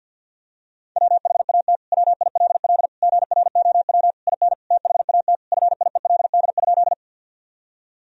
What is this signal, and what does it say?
Morse code, heard as a keyed tone, about 52 wpm, 700 Hz. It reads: WHAT WILL GROW IN THAT FIELD?
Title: Sentences from Top 1000 Words - Review Part 2 - 50wpm